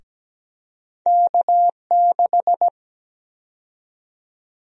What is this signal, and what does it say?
Morse code, heard as a keyed tone, about 17 wpm, 700 Hz. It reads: K6
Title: USA Call Sign Prefixes - 17wpm